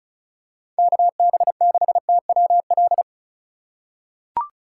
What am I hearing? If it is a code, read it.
Morse code, heard as a keyed tone, about 36 wpm, 700 Hz. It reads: KB6TWL
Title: USA Call Signs - 6 Characters - 3x Character Spacing - 35wpm